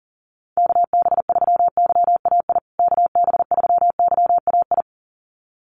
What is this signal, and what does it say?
Morse code, heard as a keyed tone, about 37 wpm, 700 Hz. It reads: KB3YAI KB3YAI
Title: USA Call Signs Repeated 2x 40wpm